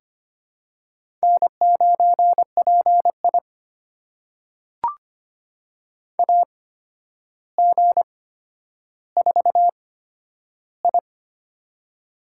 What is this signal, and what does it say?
Morse code, heard as a keyed tone, about 25 wpm, spaced as for 11 wpm, 700 Hz. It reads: N9PI AG4I
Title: USA Call Signs - 4 Characters - 8x Character Spacing - 25wpm